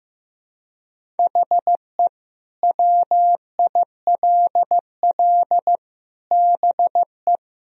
Morse code, heard as a keyed tone, about 15 wpm, 700 Hz. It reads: HE WILL BE
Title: Sentences from Top 1000 Words - Review Part 3 - 15wpm